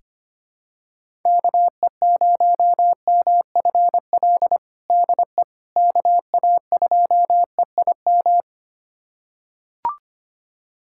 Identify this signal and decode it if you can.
Morse code, heard as a keyed tone, about 25 wpm, 700 Hz. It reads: KE0MFL DE KA2EIM
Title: USA Callsign DE Callsign - 6 Characters - 25wpm